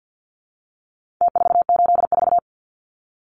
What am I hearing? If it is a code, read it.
Morse code, heard as a keyed tone, about 49 wpm, 700 Hz. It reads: T484
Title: Numbers - Contest Style - 4-Digit 50wpm